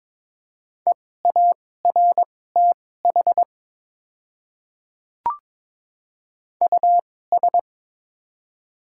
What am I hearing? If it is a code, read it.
Morse code, heard as a keyed tone, about 22 wpm, spaced as for 16 wpm, 700 Hz. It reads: EARTH US
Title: 2x Character Spacing - Top 200 Words - 22wpm